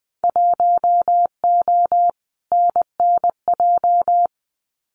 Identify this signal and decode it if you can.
Morse code, heard as a keyed tone, about 20 wpm, 700 Hz. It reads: 1O NNJ
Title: Winter Field Day Exchanges 20wpm